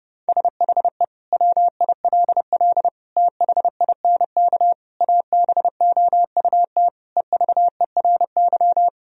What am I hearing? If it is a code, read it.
Morse code, heard as a keyed tone, about 30 wpm, 700 Hz. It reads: SHE WILL THINK ABOUT EVERY
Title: Sentences from Top 1000 Words - Review Part 1 - 30wpm